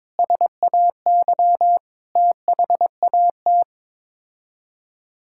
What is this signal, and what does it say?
Morse code, heard as a keyed tone, about 22 wpm, 700 Hz. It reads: SAY THAT
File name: practice / 22wpm / Sets of 8 Words - 22wpm